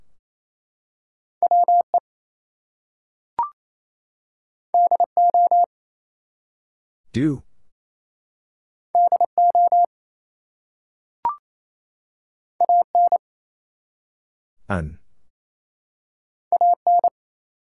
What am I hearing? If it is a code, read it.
Morse code, heard as a keyed tone, about 28 wpm, 700 Hz. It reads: WE DO DO AN AN
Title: Two Letter Words - 28wpm